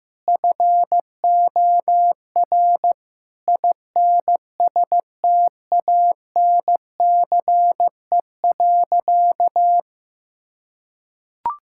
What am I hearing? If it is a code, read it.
Morse code, heard as a keyed tone, about 15 wpm, 700 Hz. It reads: FOR INSTANCE.
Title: Sentences from Top 1000 Words - Review Part 5 - 15wpm